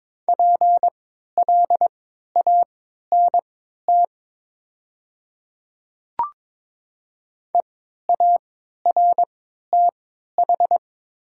Morse code, heard as a keyed tone, about 22 wpm, spaced as for 12 wpm, 700 Hz. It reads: PLANT EARTH
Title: 3x Character Spacing - Top 300 Words - 22wpm